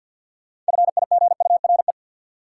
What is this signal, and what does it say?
Morse code, heard as a keyed tone, about 52 wpm, 700 Hz. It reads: FIGURE
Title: Sentences from Top 1400 Words - New Word Drill Alt - 50wpm